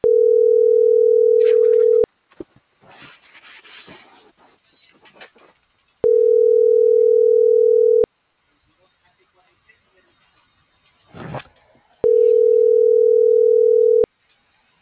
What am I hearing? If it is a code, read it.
Ringback tone (North America), 440 + 480 Hz.